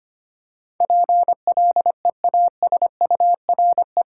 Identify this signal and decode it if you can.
Morse code, heard as a keyed tone, about 25 wpm, 690 Hz. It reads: PLEASURE
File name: 8 Letter Words - First 3 Letters Given - 25wpm